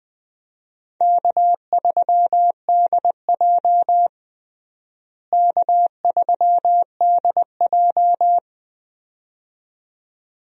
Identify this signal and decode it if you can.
Morse code, heard as a keyed tone, about 20 wpm, 700 Hz. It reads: K3DJ K3DJ
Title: USA Call Signs - 4 Characters - Repeated 2x - 20wpm